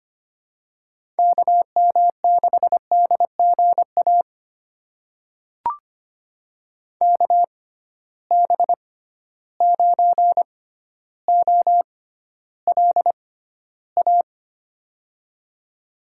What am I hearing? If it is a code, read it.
Morse code, heard as a keyed tone, about 25 wpm, spaced as for 13 wpm, 700 Hz. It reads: KM6DGA KB9OLA
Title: USA Call Signs - 6 Characters - 6x Character Spacing - 25wpm